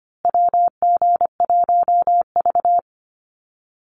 Morse code, heard as a keyed tone, about 25 wpm, 700 Hz. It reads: WG1V
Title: USA Call Signs Repeated 3x 25wpm